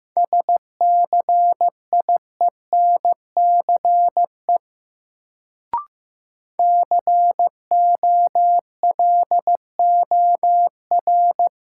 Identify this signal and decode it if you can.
Morse code, heard as a keyed tone, about 15 wpm, 700 Hz. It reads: SCIENCE COLOR
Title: Sentences from Top 1000 Words - Review Part 2 - New Word Drill - 15wpm